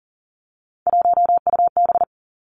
Morse code, heard as a keyed tone, about 43 wpm, 700 Hz. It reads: 1UB
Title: Three Challenging Letter-Number Combinations 40wpm